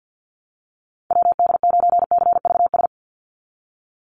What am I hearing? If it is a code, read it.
Morse code, heard as a keyed tone, about 50 wpm, 700 Hz. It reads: WD9CVS
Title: USA Call Signs 50wpm